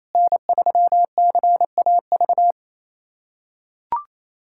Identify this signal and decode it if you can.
Morse code, heard as a keyed tone, about 28 wpm, 700 Hz. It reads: N3CAV
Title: USA Callsign DE Callsign 28wpm